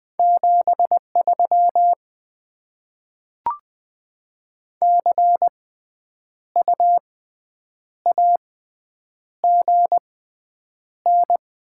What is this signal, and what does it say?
Morse code, heard as a keyed tone, about 20 wpm, spaced as for 8 wpm, 700 Hz. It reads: 73 CU AGN
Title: Top 20 QSO Elements - 6x Character Spacing - 20wpm